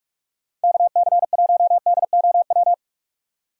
Morse code, heard as a keyed tone, about 41 wpm, 700 Hz. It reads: KC1DOW
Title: USA Call Signs - 6 Characters - 6x Character Spacing - 45wpm